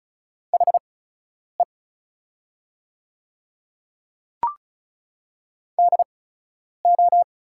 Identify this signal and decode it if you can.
Morse code, heard as a keyed tone, about 36 wpm, spaced as for 10 wpm, 700 Hz. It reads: HE DO
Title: 8x Character Spacing - Top 100 Words - 35wpm